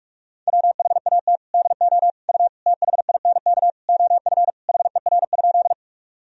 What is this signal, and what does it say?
Morse code, heard as a keyed tone, about 41 wpm, 700 Hz. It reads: WHAT DO U THINK OF HER?
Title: Sentences from Top 200 Words and Abbreviations 45wpm